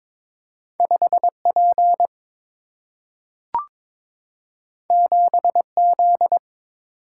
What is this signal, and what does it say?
Morse code, heard as a keyed tone, about 22 wpm, 700 Hz. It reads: 5 P 7 Z
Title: Single-Letter Single-Number 22wpm